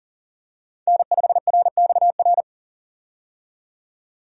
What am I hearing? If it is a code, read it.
Morse code, heard as a keyed tone, about 40 wpm, 700 Hz. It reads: N5RXR
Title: USA Call Signs - 5 Characters - 40wpm